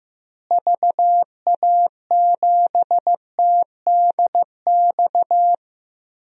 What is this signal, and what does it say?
Morse code, heard as a keyed tone, about 15 wpm, 700 Hz. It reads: VA7TDX
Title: DXCC-100 Call Signs - 6 Characters - Repeated 2x - 15wpm